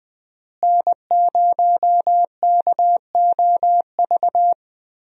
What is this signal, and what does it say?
Morse code, heard as a keyed tone, about 20 wpm, 700 Hz. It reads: N0KOV
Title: USA Call Signs - 5 Characters - Repeated 3x - 20wpm